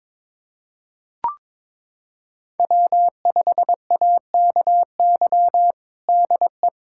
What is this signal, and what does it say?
Morse code, heard as a keyed tone, about 21 wpm, 700 Hz. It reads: W5AKY DE
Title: USA Callsign DE Callsign - 5 Characters - 22wpm